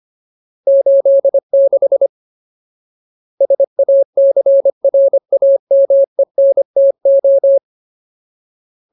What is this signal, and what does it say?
Morse code, heard as a keyed tone, about 25 wpm, 550 Hz. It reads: 86 SACRAMENTO
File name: Ultimate Word List / 25WPM Ultimate